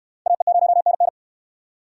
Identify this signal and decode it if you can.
Morse code, heard as a keyed tone, about 69 wpm, 700 Hz. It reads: AE0NN